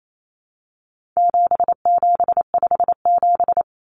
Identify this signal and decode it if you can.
Morse code, heard as a keyed tone, about 28 wpm, 700 Hz. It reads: 7757
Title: Numbers - 4-Digit 28wpm